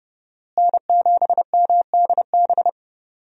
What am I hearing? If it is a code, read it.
Morse code, heard as a keyed tone, about 30 wpm, 700 Hz. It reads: N7MDB
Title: USA Call Signs - 5 Characters - Repeated 3x - 30wpm